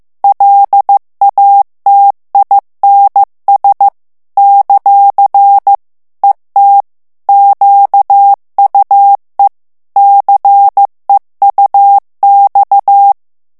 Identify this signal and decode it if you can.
Morse code, heard as a keyed tone, about 15 wpm, 800 Hz. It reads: LATINS ; ET QUE CEUX
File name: FRANCAIS TEXTE CLAIR 2 VITESSE 14 MOTS MINUTES